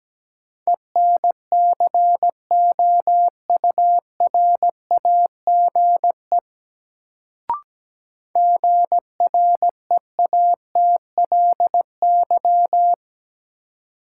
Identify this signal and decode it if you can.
Morse code, heard as a keyed tone, about 17 wpm, 700 Hz. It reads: ENCOURAGE GREATLY
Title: Sentences from Top 1400 Words - New Word Drill Alt - 17wpm